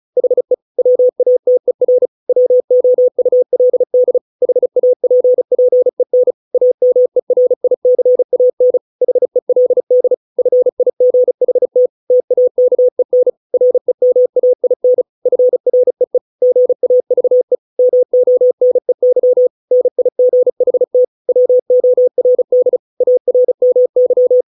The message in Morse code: HE WATER WOULD HAPPEN AMERICAN HELD FIGHT TAKEN REMAIN FREE GAVE MONEY NIGHT WORD ARMY

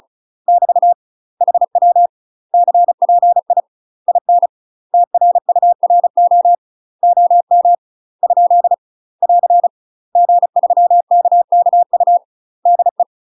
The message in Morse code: <BT> HW CPI IN TRURO OM ? <AR> G3KKU DE

